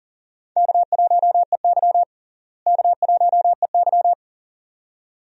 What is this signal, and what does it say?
Morse code, heard as a keyed tone, about 38 wpm, 700 Hz. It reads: K1EY K1EY
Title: USA Call Signs - 4 Characters - Repeated 2x - 40wpm